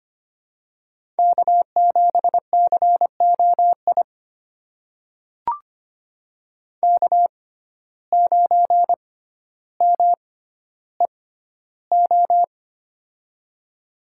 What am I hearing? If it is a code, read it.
Morse code, heard as a keyed tone, about 25 wpm, spaced as for 13 wpm, 700 Hz. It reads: K7COI K9MEO
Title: USA Call Signs - 5 Characters - 6x Character Spacing - 25wpm